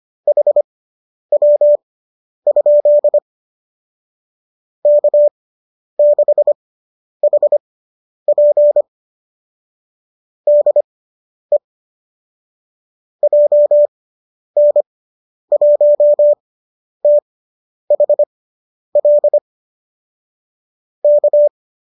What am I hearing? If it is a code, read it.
Morse code, heard as a keyed tone, about 25 wpm, spaced as for 10 wpm, 590 Hz. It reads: HW? K6HP DE JN1THL K